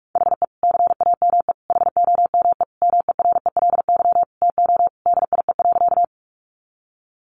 Morse code, heard as a keyed tone, about 44 wpm, 700 Hz. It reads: HE CAME HOME MERELY TO DIE.
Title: Sentences from Top 1400 Words 45wpm